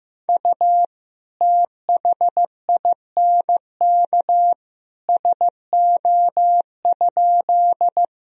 Morse code, heard as a keyed tone, about 16 wpm, 700 Hz. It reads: U THINK SO?
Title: Sentences from Top 200 Words and Abbreviations 15wpm